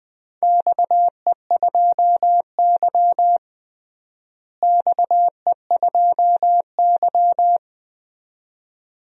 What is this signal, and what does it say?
Morse code, heard as a keyed tone, about 20 wpm, 700 Hz. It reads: XE2Y XE2Y
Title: DXCC-100 Call Signs - 4 Characters - Repeated 2x - 20wpm